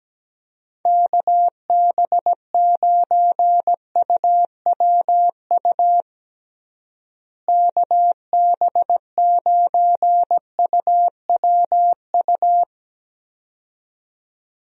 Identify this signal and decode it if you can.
Morse code, heard as a keyed tone, about 17 wpm, 700 Hz. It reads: KB9UWU KB9UWU
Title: USA Call Signs - 6 Characters - Repeated 2x - 17wpm